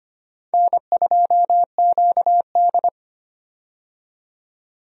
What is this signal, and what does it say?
Morse code, heard as a keyed tone, about 25 wpm, 700 Hz. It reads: N2QD
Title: USA Call Signs - 4 Characters - 25wpm